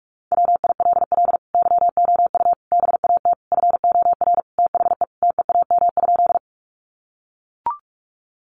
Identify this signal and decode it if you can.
Morse code, heard as a keyed tone, about 44 wpm, 700 Hz. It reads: WILL YOU BAT FOR THE TEAM?
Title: Sentences from Top 1500 Words 45wpm